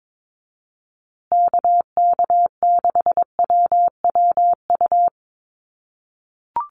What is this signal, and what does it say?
Morse code, heard as a keyed tone, about 22 wpm, 700 Hz. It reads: KK6WWU
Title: USA Call Signs Repeated 3x 22wpm